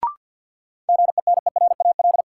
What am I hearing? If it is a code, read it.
Morse code, heard as a keyed tone, about 52 wpm, 700 Hz. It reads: GENERAL
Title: Sentences from Top 600 Words - New Word Drill - 50wpm